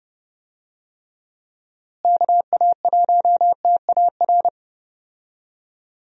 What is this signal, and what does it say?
Morse code, heard as a keyed tone, about 30 wpm, 700 Hz. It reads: KA1TAR
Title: USA Call Signs - 6 Characters - Repeated 2x - 30wpm